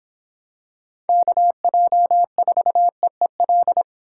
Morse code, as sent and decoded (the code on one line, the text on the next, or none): -.- .--- ....- . .-..
KJ4EEL